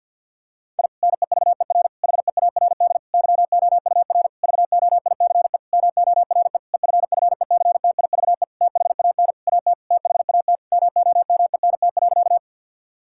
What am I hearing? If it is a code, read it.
Morse code, heard as a keyed tone, about 52 wpm, 700 Hz. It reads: I NEVER HEARD YOUR VOICE MORE EFFECTIVE THAN AT THAT MOMENT.
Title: Sentences from Top 1000 Words - Review Part 6 - 50wpm